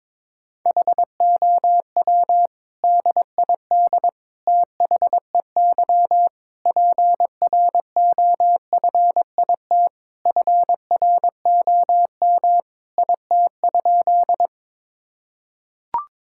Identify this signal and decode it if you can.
Morse code, heard as a keyed tone, about 22 wpm, 700 Hz. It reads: HOW DID THEY PROFIT FROM IT?
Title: Sentences from Top 1000 Words - Review Part 5 - 22wpm